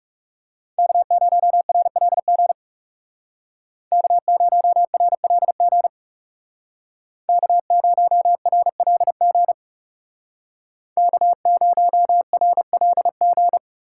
Morse code, heard as a keyed tone, about 36 wpm, 700 Hz. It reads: K0RLG K0RLG K0RLG K0RLG
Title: Speed Racing - USA 1x3 Call Signs 30wpm